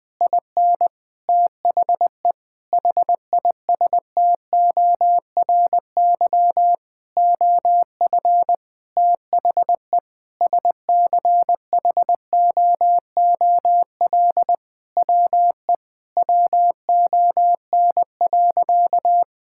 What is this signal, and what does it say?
Morse code, heard as a keyed tone, about 20 wpm, 700 Hz. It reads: IN THE HISTORY OF THE SCHOOL WE WON.
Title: Sentences with 9-Word Phrases - 20wpm